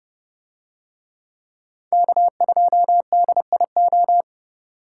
Morse code, heard as a keyed tone, about 28 wpm, 700 Hz. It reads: K2DIO